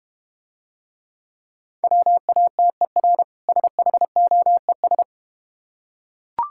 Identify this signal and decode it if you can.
Morse code, heard as a keyed tone, about 31 wpm, 690 Hz. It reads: WATER SHOES
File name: practice / 25wpm / Difficult Words - Speed-Racing with Hints - 25wpm